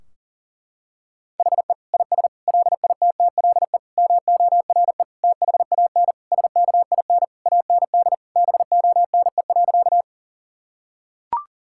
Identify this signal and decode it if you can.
Morse code, heard as a keyed tone, about 40 wpm, 700 Hz. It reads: HE IS LITTLE MORE THAN SKIN AND BONE.
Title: Sentences from Top 1600 Words 40wpm